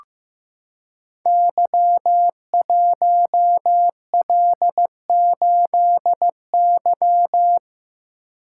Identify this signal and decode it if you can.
Morse code, heard as a keyed tone, about 15 wpm, 700 Hz. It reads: Y1L8Y
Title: Code Groups - 5 Letter:Numbers - 15wpm